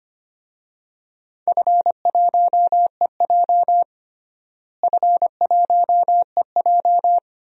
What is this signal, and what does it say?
Morse code, heard as a keyed tone, about 25 wpm, 700 Hz. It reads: F1EJ F1EJ
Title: DXCC-100 Call Signs - 4 Characters - Repeated 2x - 25wpm